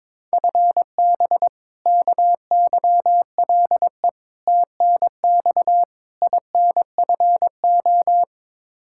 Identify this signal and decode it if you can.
Morse code, heard as a keyed tone, about 22 wpm, 700 Hz. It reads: FB KYLE TNX INFO
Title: Practice QSO - Exchange 3 - More Info - 22wpm